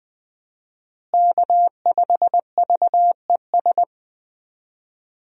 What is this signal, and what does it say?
Morse code, heard as a keyed tone, about 20 wpm, 700 Hz. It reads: K5VES
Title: USA Call Signs - 5 Characters - Repeated 3x - 20wpm